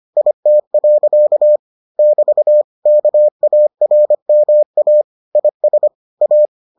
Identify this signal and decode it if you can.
Morse code, heard as a keyed tone, about 25 wpm, 600 Hz. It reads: IT. <BT> KARMA IS A